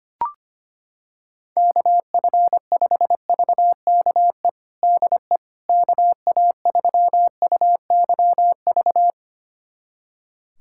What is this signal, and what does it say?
Morse code, heard as a keyed tone, about 25 wpm, 700 Hz. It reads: KF5VKE DE KA3UYV
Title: USA Callsign DE Callsign - 6 Characters - 25wpm